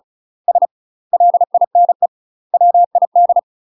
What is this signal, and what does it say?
Morse code, heard as a keyed tone, about 36 wpm, 700 Hz. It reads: S LINE WID
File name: UK MASTER QSOs TRAINING / 35WPM / UK MASTER QSOs 0017